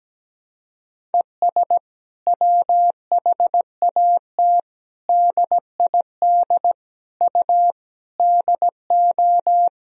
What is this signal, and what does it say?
Morse code, heard as a keyed tone, about 17 wpm, 700 Hz. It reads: ES WHAT DID U DO
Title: Sentences from Top 100 Words and Abbreviations 17wpm